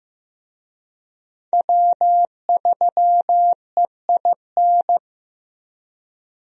Morse code, heard as a keyed tone, about 16 wpm, 700 Hz. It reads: W3EIN
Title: USA Call Signs - 5 Characters - Repeated 3x - 15wpm